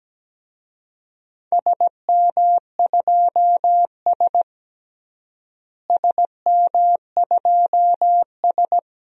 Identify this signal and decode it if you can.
Morse code, heard as a keyed tone, about 17 wpm, 700 Hz. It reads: SM2S SM2S
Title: DXCC-100 Call Signs - 4 Characters - Repeated 3x - 17wpm